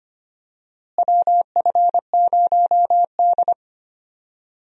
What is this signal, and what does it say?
Morse code, heard as a keyed tone, about 25 wpm, 700 Hz. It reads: WF0D
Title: USA Call Signs - 4 Characters - Repeated 3x - 25wpm